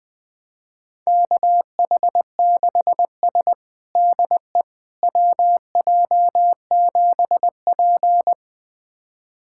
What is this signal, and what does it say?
Morse code, heard as a keyed tone, about 21 wpm, 700 Hz. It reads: KH6S DE WJ7P
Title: USA Callsign DE Callsign - 4 Characters - 20wpm